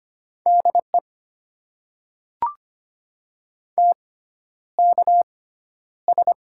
Morse code, heard as a keyed tone, about 25 wpm, spaced as for 11 wpm, 700 Hz. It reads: DE TKS
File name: Top 90 QSO Elements - 6x Character Spacing - 25wpm